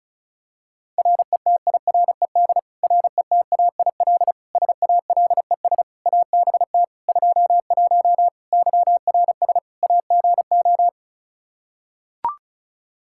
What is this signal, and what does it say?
Morse code, heard as a keyed tone, about 34 wpm, 700 Hz. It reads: RETIRED RETAIL SALES ABT 21 YRS AGO
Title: Practice QSO - Exchange 3 - More Info - 35wpm